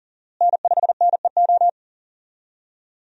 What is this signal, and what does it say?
Morse code, heard as a keyed tone, about 40 wpm, 700 Hz. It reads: N5NEO
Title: USA Call Signs - 5 Characters - Repeated 2x - 40wpm